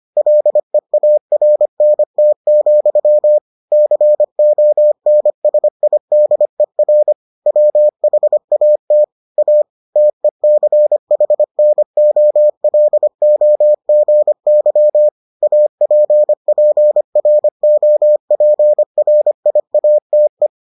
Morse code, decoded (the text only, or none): LEARNT, CONSIDER WHAT A TECHNOLOGY APPROPRIATE